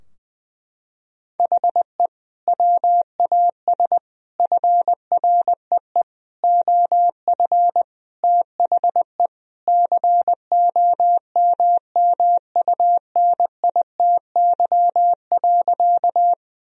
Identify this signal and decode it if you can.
Morse code, heard as a keyed tone, about 20 wpm, 700 Hz. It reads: HE WAS FREE OF THE COMMUNITY.
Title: Sentences from Top 1000 Words - Review Part 4 - 20wpm